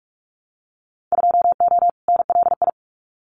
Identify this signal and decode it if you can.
Morse code, heard as a keyed tone, about 44 wpm, 700 Hz. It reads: 2O NLI
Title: Winter Field Day Exchanges 45wpm